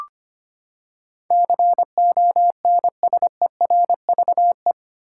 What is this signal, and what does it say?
Morse code, heard as a keyed tone, about 25 wpm, 700 Hz. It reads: CONSERVE
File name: Eight Letter Words - 25wpm